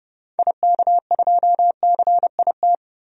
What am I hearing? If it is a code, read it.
Morse code, heard as a keyed tone, about 30 wpm, 700 Hz. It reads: IK2CIT
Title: DXCC-100 Call Signs - 6 Characters - 30wpm